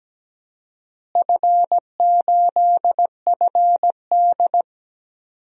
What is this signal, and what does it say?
Morse code, heard as a keyed tone, about 17 wpm, 700 Hz. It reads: F8FD
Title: DXCC-100 Call Signs - 4 Characters - Repeated 3x - 17wpm